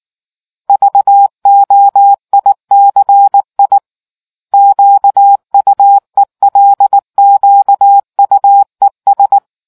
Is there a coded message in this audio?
Yes: Morse code from 0.7 to 9.4 s, heard as a keyed tone, about 19 wpm, 800 Hz. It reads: VOICI QUELQUES